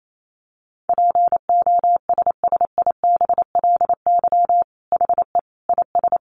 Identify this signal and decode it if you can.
Morse code, heard as a keyed tone, about 28 wpm, 700 Hz. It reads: POSSIBLY HE IS
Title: Sentences from Top 1300 Words 28wpm